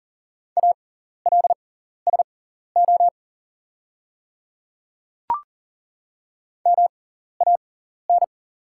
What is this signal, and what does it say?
Morse code, heard as a keyed tone, about 40 wpm, spaced as for 14 wpm, 690 Hz. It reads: ALSO MAN
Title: 6x Character Spacing - Top 200 Words - 40wpm